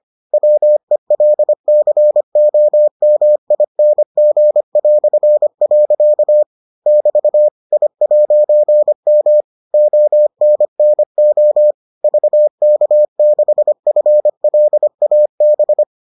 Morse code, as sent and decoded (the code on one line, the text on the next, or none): .-- . .-.. -.-. --- -- .. -. --. .-..-. .-.-.- / -...- / .. .----. -- / --- -. -. --- / ...- -.- -.... ..-. .-.. .- -...
WELCOMING". <BT> I'M ONNO VK6FLAB